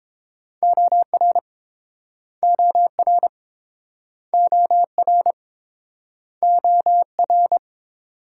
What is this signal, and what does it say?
Morse code, heard as a keyed tone, about 29 wpm, 700 Hz. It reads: OR OR OR OR